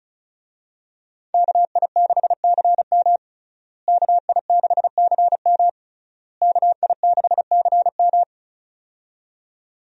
Morse code, heard as a keyed tone, about 36 wpm, 700 Hz. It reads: KI6CM KI6CM KI6CM
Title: USA Call Signs - 5 Characters - Repeated 3x - 35wpm